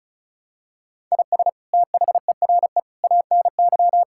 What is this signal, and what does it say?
Morse code, heard as a keyed tone, about 34 wpm, 700 Hz. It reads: IS THERE ANY
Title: Sentences from Top 1000 Words - Review Part 6 - 35wpm